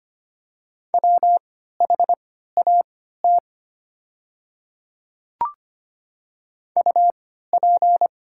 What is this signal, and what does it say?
Morse code, heard as a keyed tone, about 25 wpm, spaced as for 14 wpm, 690 Hz. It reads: WHAT UP
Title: 3x Character Spacing - Top 100 Words - 25wpm